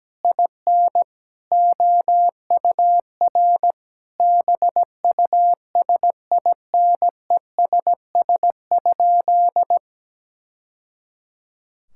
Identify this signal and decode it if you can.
Morse code, heard as a keyed tone, about 17 wpm, 700 Hz. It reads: IN OUR BUSINESS?
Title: Sentences from Top 1000 Words - Review Part 6 - 17wpm